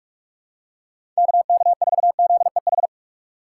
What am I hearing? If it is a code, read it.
Morse code, heard as a keyed tone, about 46 wpm, 690 Hz. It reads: KK4ZEH
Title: USA Call Signs - 6 Characters - 2x Character Spacing - 45wpm